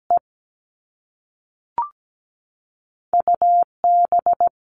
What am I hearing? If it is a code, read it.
Morse code, heard as a keyed tone, about 17 wpm, 700 Hz. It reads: E UB
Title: All Cut Numbers - 2-Digit 17wpm